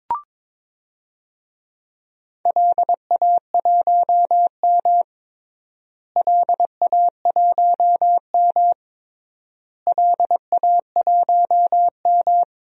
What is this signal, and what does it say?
Morse code, heard as a keyed tone, about 22 wpm, 700 Hz. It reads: LA1M LA1M LA1M
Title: DXCC-100 Call Signs - 4 Characters - Repeated 3x - 22wpm